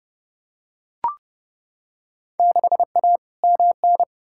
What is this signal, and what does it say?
Morse code, heard as a keyed tone, about 32 wpm, 700 Hz. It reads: 6A MN
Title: Field Day Exchanges Encore - 30wpm